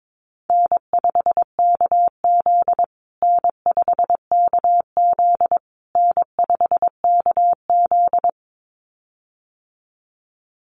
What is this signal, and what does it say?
Morse code, heard as a keyed tone, about 22 wpm, 690 Hz. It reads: N5KZ N5KZ N5KZ
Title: USA Call Signs Repeated 3x 22wpm